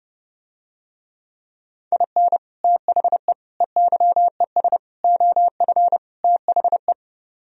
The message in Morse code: IN THE EYES OF THE

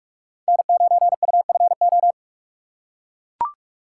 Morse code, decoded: N9UFO

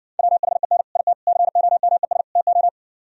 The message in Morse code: WHEN IT COMES TO

